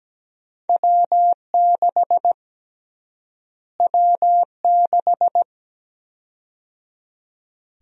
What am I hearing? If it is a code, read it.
Morse code, heard as a keyed tone, about 17 wpm, 700 Hz. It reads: W6 W6
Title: USA Call Sign Prefixes - Repeated 2x - 17wpm